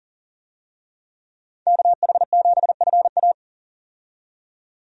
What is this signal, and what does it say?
Morse code, heard as a keyed tone, about 40 wpm, 700 Hz. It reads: KH7FA
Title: USA Call Signs - 5 Characters - 40wpm